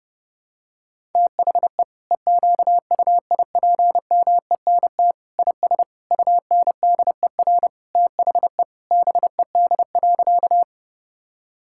Morse code, heard as a keyed tone, about 30 wpm, 700 Hz. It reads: THE EQUIPMENT IS UNDER THE BED.
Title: Sentences from Top 1000 Words - Review Part 5 - 30wpm